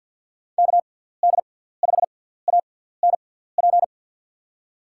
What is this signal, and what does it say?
Morse code, heard as a keyed tone, about 47 wpm, spaced as for 17 wpm, 700 Hz. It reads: KD5ANP
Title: USA Call Signs - 6 Characters - 6x Character Spacing - 50wpm